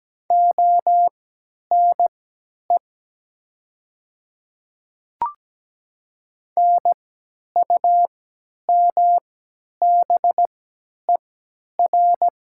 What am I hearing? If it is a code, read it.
Morse code, heard as a keyed tone, about 17 wpm, spaced as for 10 wpm, 700 Hz. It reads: ONE NUMBER
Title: 3x Character Spacing - Top 300 Words - 17wpm